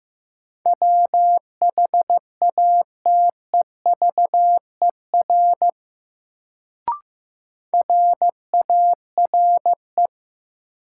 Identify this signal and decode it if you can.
Morse code, heard as a keyed tone, about 15 wpm, 700 Hz. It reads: WHATEVER RARE
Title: Sentences from Top 1100 Words - New Word Drill Alt - 15wpm